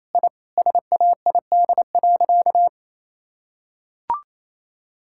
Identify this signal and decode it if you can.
Morse code, heard as a keyed tone, about 27 wpm, 700 Hz. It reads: I SAID.
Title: Sentences from Top 1000 Words - Review Part 6 - 28wpm